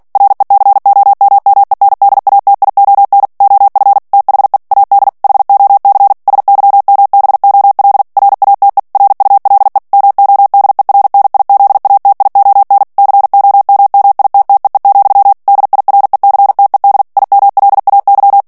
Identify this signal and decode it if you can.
Morse code, heard as a keyed tone, about 48 wpm, 800 Hz. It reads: RECOMMENDATION OF THE AD HOC SYMBOL RATE RULE MODERNIZATION COMMITTEE, DIRECTED IMLAY
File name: ANGLAIS texte clair 3 vitesse 45 mots minute